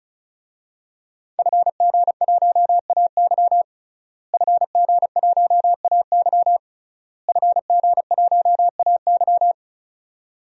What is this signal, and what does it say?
Morse code, heard as a keyed tone, about 36 wpm, 700 Hz. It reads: FG1AY FG1AY FG1AY
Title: DXCC-100 Call Signs - 5 Characters - Repeated 3x - 35wpm